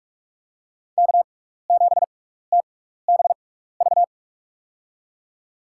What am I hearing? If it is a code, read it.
Morse code, heard as a keyed tone, about 47 wpm, spaced as for 16 wpm, 700 Hz. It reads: K7TBV
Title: USA Call Signs - 5 Characters - 6x Character Spacing - 45wpm